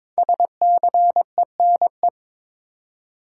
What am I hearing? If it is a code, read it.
Morse code, heard as a keyed tone, about 22 wpm, 700 Hz. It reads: SCENE
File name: 5 Letter Words - First 3 Letters Given - 22wpm